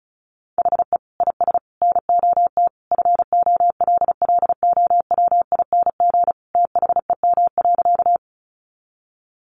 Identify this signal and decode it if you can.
Morse code, heard as a keyed tone, about 34 wpm, 700 Hz. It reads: HE IS NOT FOLLOWING THEM.
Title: Sentences from Top 700 Words 35wpm